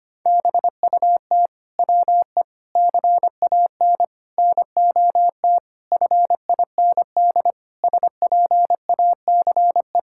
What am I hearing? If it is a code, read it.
Morse code, heard as a keyed tone, about 25 wpm, 700 Hz. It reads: BUT WE CAN NOT FIND SPACE
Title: Sentences from Top 1000 Words - Review Part 5 - 25wpm